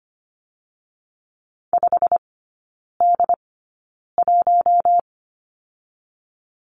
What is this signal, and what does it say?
Morse code, heard as a keyed tone, about 25 wpm, spaced as for 9 wpm, 700 Hz. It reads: 5D1
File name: Lesson 15e - Number 5 - Three Characters 25wpm